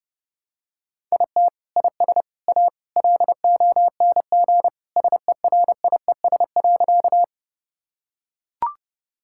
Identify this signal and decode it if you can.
Morse code, heard as a keyed tone, about 29 wpm, 700 Hz. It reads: IT IS A LONG SERIES.
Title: Sentences from Top 1000 Words - Review Part 5 - 30wpm